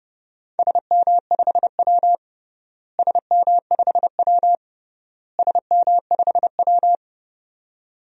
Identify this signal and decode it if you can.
Morse code, heard as a keyed tone, about 30 wpm, 700 Hz. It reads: SM5W SM5W SM5W
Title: DXCC-100 Call Signs - 4 Characters - Repeated 3x - 30wpm